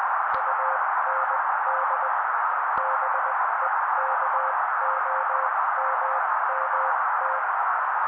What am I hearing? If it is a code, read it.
Morse code, heard as a keyed tone, about 20 wpm, spaced as for 16 wpm, 530 Hz. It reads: UND BEKOMMT
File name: daily / 16wpm noise